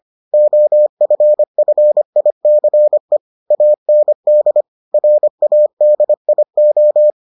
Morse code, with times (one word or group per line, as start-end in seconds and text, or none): 0.3-3.2 OFFICE
3.5-4.6 AND
4.9-7.1 RADIO